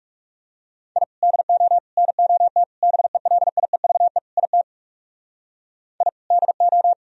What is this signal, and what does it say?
Morse code, heard as a keyed tone, about 39 wpm, 700 Hz. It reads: I DO NOT BELIEVE IT I DO